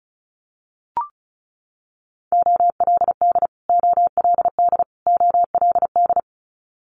Morse code, heard as a keyed tone, about 36 wpm, 700 Hz. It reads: OLD OLD OLD
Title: Multi-letter Phonemes Repeated 3x 35wpm